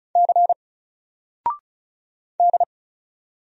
Morse code, heard as a keyed tone, about 36 wpm, 700 Hz. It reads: C D
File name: Single-Letter-Number-35wpm